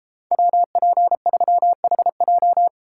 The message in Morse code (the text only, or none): WP3HJ